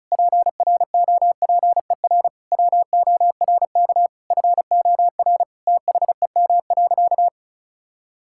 Morse code, heard as a keyed tone, about 36 wpm, 700 Hz. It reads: PROPER WORK FOR THEM.
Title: Sentences from Top 1000 Words - Review Part 5 - 35wpm